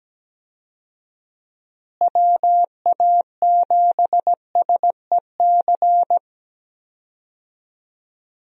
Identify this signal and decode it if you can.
Morse code, heard as a keyed tone, about 17 wpm, 700 Hz. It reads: WA7SEC
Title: USA Call Signs - 6 Characters - Repeated 3x - 17wpm